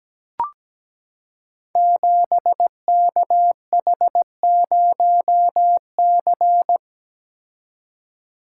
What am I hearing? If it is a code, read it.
Morse code, heard as a keyed tone, about 17 wpm, 700 Hz. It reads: 7KH0C
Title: Code Groups - 5 Letter:Numbers - 17wpm